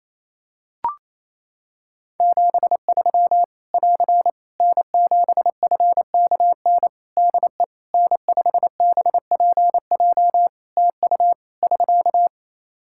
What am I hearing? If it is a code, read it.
Morse code, heard as a keyed tone, about 28 wpm, 700 Hz. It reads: 73 <AR> N7FKN DE N5BPJ TU <SK>
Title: Practice QSO - Exchange 4 - Ending - 28wpm